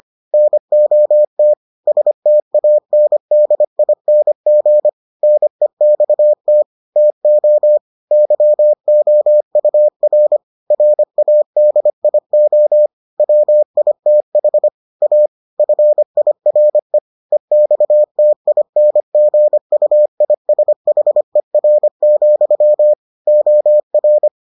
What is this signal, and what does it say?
Morse code, heard as a keyed tone, about 25 wpm, 600 Hz. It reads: NOT STANDING NEXT TO YOUR RADIO WITH A FIRE EXTINGUISHER, OR